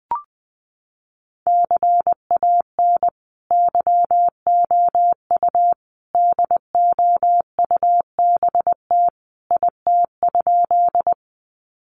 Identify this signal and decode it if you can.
Morse code, heard as a keyed tone, about 20 wpm, 700 Hz. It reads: CAN YOU DOUBT IT?
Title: Sentences from Top 1500 Words 20wpm